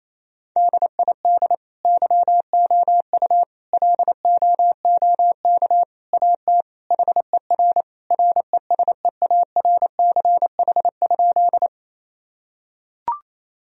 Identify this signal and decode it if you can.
Morse code, heard as a keyed tone, about 27 wpm, 700 Hz. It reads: DID YOU LOOK AT HER RESEARCH?
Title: Sentences from Top 1000 Words - Review Part 4 - 28wpm